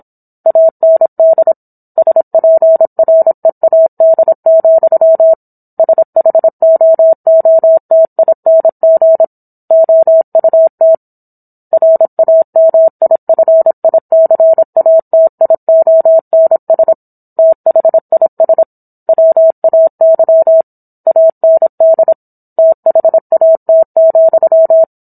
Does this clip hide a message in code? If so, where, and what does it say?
Yes: Morse code from 0.5 to 24.8 s, heard as a keyed tone, about 27 wpm, 650 Hz. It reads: AND SPREAD, SHOOTING OUT RAMIFICATIONS THIS WAY AND THAT,